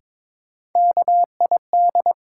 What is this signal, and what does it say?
Morse code, heard as a keyed tone, about 22 wpm, 700 Hz. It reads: KID
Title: Sets of 5 Words - New Word Drill - 22wpm